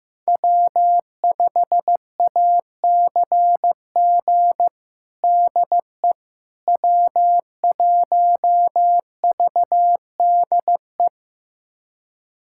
Morse code, heard as a keyed tone, about 16 wpm, 700 Hz. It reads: W5ACG DE W1VDE
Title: USA Callsign DE Callsign - 5 Characters - 15wpm